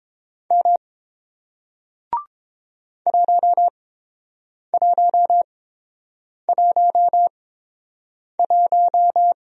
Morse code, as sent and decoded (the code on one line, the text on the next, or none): -- / .---- .---- .---- .----
M 1111